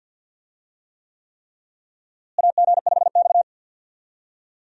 Morse code, heard as a keyed tone, about 48 wpm, 700 Hz. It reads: AG5X
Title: USA Call Signs - 4 Characters - Repeated 3x - 50wpm